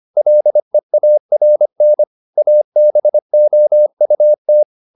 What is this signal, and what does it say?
Morse code, heard as a keyed tone, about 25 wpm, 600 Hz. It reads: LEARN ABOUT
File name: the old man and the sea part 1 25wpm